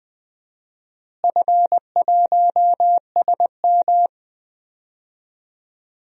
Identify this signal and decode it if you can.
Morse code, heard as a keyed tone, about 21 wpm, 700 Hz. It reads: F1SM
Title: DXCC-100 Call Signs - 4 Characters - Repeated 3x - 20wpm